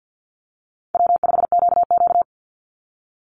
Morse code, heard as a keyed tone, about 50 wpm, 700 Hz. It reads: W5QQ
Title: Easy Call Signs Repeated 3x - 50wpm